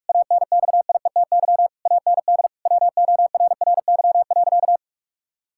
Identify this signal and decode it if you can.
Morse code, heard as a keyed tone, about 48 wpm, 700 Hz. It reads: ANXIETY AND WORRY.
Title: Pangram Challenge v2 - 45wpm